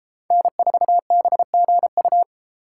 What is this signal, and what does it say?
Morse code, heard as a keyed tone, about 33 wpm, 700 Hz. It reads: N4BGU